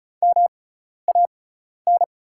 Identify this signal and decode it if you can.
Morse code, heard as a keyed tone, about 34 wpm, spaced as for 12 wpm, 690 Hz. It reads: MAN
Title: 6x Character Spacing - Top 300 Words - 35wpm